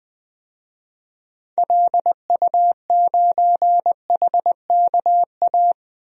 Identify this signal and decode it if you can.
Morse code, heard as a keyed tone, about 20 wpm, 700 Hz. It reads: LU9HKA
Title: DXCC-100 Call Signs - 6 Characters - 20wpm